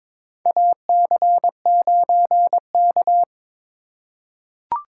Morse code, heard as a keyed tone, about 22 wpm, 700 Hz. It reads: AC9K
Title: USA Call Signs - 4 Characters - 4x Character Spacing - 22wpm